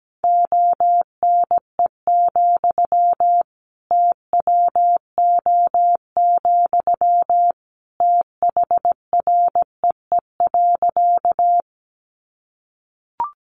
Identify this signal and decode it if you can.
Morse code, heard as a keyed tone, about 17 wpm, 700 Hz. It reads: ONE, TWO, THREE.
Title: Sentences from Top 200 Words 17wpm